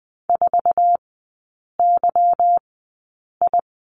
Morse code, heard as a keyed tone, about 20 wpm, spaced as for 8 wpm, 700 Hz. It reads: 4YI
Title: 152 Lesson 25e - Letter p - Three Characters 20wpm